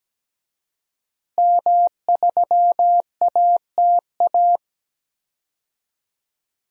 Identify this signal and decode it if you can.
Morse code, heard as a keyed tone, about 17 wpm, 700 Hz. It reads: M3ATA
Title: DXCC-100 Call Signs - 5 Characters - Repeated 3x - 17wpm